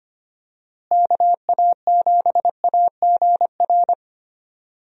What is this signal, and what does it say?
Morse code, heard as a keyed tone, about 25 wpm, 700 Hz. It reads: KA7AGR
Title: USA Call Signs - 6 Characters - Repeated 2x - 25wpm